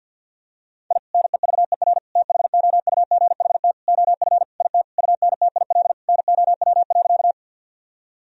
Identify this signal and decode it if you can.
Morse code, heard as a keyed tone, about 52 wpm, 700 Hz. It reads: I NEVER THOUGHT OF IT UNTIL NOW.
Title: Sentences from Top 1000 Words - Review Part 1 - 50wpm